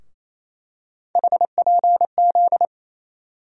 Morse code, heard as a keyed tone, about 28 wpm, 700 Hz. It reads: HPZ
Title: USA Call Sign Postfixes - 3 Letters 28wpm